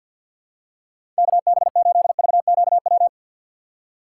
Morse code, heard as a keyed tone, about 48 wpm, 700 Hz. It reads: KB8VQW
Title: USA Call Signs - 6 Characters - 3x Character Spacing - 50wpm